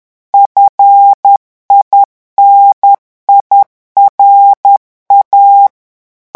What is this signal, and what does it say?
Morse code, heard as a keyed tone, about 10 wpm, 800 Hz. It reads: FINIRA
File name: morse mp3 / revision francais 10 mots minute